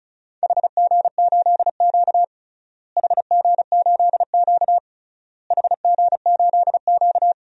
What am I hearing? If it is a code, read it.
Morse code, heard as a keyed tone, about 36 wpm, 700 Hz. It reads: HG8Q HG8Q HG8Q
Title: DXCC-100 Call Signs - 4 Characters - Repeated 3x - 35wpm